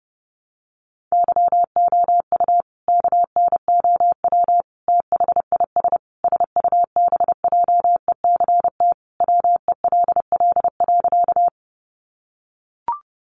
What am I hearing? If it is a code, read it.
Morse code, heard as a keyed tone, about 30 wpm, 700 Hz. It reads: YOU KNOW THIS SUBJECT WELL.
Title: Sentences from Top 800 Words 30wpm